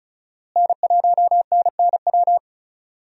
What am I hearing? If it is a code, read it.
Morse code, heard as a keyed tone, about 34 wpm, 700 Hz. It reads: N1NNW
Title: USA Call Signs - 5 Characters - Repeated 3x - 35wpm